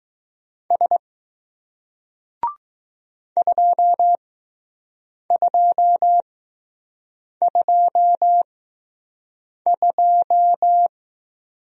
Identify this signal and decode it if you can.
Morse code, heard as a keyed tone, about 20 wpm, spaced as for 6 wpm, 700 Hz. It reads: S 2222